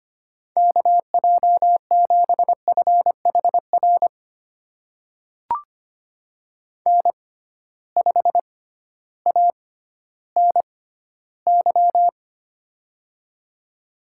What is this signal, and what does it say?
Morse code, heard as a keyed tone, about 25 wpm, spaced as for 14 wpm, 700 Hz. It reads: KJ7FHR N5ANY
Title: USA Call Signs - 6x Character Spacing - 25wpm